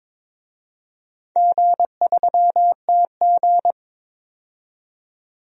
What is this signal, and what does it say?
Morse code, heard as a keyed tone, about 23 wpm, 690 Hz. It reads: G3TG
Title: DXCC-100 Call Signs - 4 Characters - 22wpm